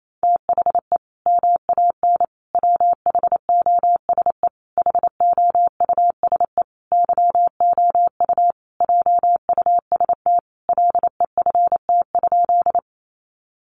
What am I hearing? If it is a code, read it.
Morse code, heard as a keyed tone, about 28 wpm, 700 Hz. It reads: THE MAN WHOSE HOUSE YOU JUST LEFT?
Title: Sentences from Top 1200 Words 28wpm